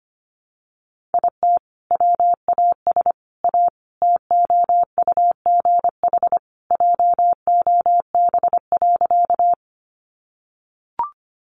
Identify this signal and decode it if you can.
Morse code, heard as a keyed tone, about 25 wpm, 700 Hz. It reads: IT WAS A TOUGH JOB.
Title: Sentences from Top 1200 Words 25wpm